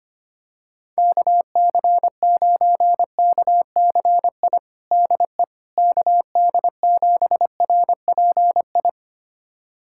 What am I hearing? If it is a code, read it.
Morse code, heard as a keyed tone, about 25 wpm, 700 Hz. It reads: KC9KCI DE KD7RPI
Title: USA Callsign DE Callsign 25wpm